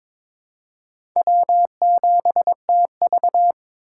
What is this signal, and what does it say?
Morse code, heard as a keyed tone, about 26 wpm, 700 Hz. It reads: W7TV
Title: Speed Racing - USA 1x2 Call Signs 22wpm